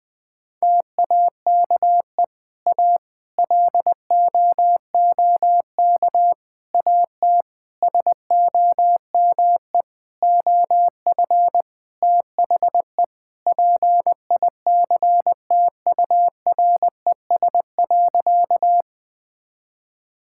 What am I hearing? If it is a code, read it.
Morse code, heard as a keyed tone, about 20 wpm, 700 Hz. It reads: TAKE A LOOK AT SOME OF THE PICTURES.
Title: Sentences with 7-Word Phrases - 20wpm